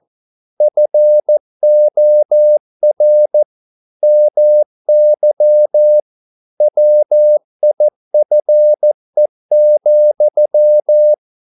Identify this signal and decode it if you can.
Morse code, heard as a keyed tone, about 14 wpm, 600 Hz. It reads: FOR MY WIFE,
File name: mp3 files / a dogs mistake - 14wpm